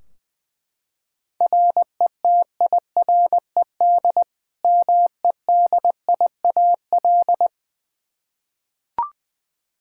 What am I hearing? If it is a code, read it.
Morse code, heard as a keyed tone, about 20 wpm, 700 Hz. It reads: RETIRED MEDIAL
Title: Practice QSO - Exchange 3 - More Info - 20wpm